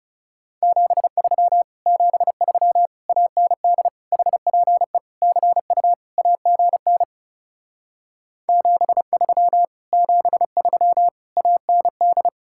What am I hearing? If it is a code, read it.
Morse code, heard as a keyed tone, about 40 wpm, 700 Hz. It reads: 73 73 AND HPE CU AGN 73 73 AND